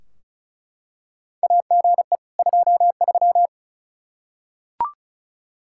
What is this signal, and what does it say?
Morse code, heard as a keyed tone, about 34 wpm, 700 Hz. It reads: AGE 23
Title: K1USN SST QSO Practice 35wpm